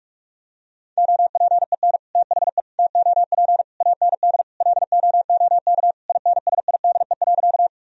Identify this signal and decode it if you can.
Morse code, heard as a keyed tone, about 46 wpm, 700 Hz. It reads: OPEN THE TOP AND LOOK INSIDE.
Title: Sentences from Top 1000 Words - Review Part 2 - 45wpm